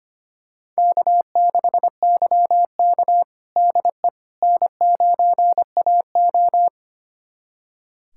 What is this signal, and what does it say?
Morse code, heard as a keyed tone, about 25 wpm, 700 Hz. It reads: K6YK DE N9AO
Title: USA Callsign DE Callsign - 4 Characters - 25wpm